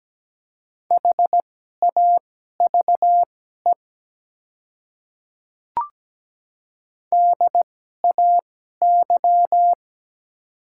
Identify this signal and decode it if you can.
Morse code, heard as a keyed tone, about 17 wpm, spaced as for 12 wpm, 700 Hz. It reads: HAVE DAY
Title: 2x Character Spacing - Top 100 Words - 17wpm